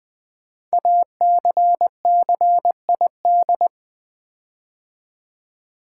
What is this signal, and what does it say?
Morse code, heard as a keyed tone, about 20 wpm, 700 Hz. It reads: ACCID
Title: Build A Word - 5 Letters At A Time - 12-Letter Words - 20wpm